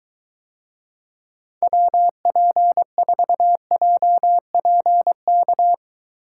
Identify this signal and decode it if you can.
Morse code, heard as a keyed tone, about 22 wpm, 700 Hz. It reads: WP4JPK